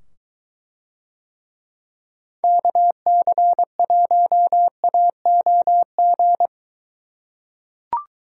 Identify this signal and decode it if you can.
Morse code, heard as a keyed tone, about 23 wpm, 700 Hz. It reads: KC1AOG